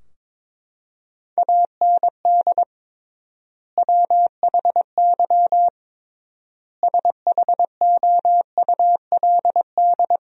Morse code, heard as a keyed tone, about 22 wpm, 700 Hz. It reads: AND WHY SHOULD
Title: 3x Word-Spacing - Sentences from Top 300 Words - 22wpm